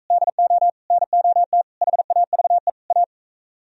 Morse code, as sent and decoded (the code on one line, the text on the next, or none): -.. --- / -. --- - / .... .- ...- . / .-
DO NOT HAVE A